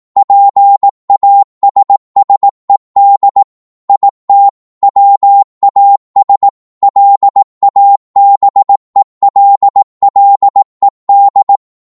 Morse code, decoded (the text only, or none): PASSED IT WAS LABELLED